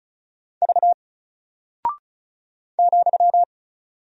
Morse code, heard as a keyed tone, about 36 wpm, 700 Hz. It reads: V ,
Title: Single Character (Letter:Number:Punctuation:Prosign) 35wpm